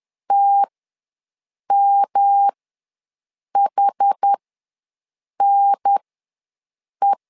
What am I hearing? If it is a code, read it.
Morse code, heard as a keyed tone, about 10 wpm, spaced as for 6 wpm, 800 Hz. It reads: TMHNE
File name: LEZIONI / 3 MWRO / 3 M ws 10wpm 2121